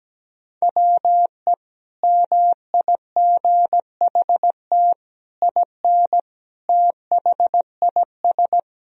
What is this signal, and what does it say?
Morse code, heard as a keyed tone, about 17 wpm, 700 Hz. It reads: WE MIGHT IN THIS
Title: Sentences from Top 1000 Words - Review Part 1 - 17wpm